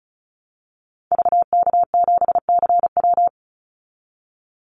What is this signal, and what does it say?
Morse code, heard as a keyed tone, about 34 wpm, 700 Hz. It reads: VK7CW
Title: Call Signs Repeated 2x - 35wpm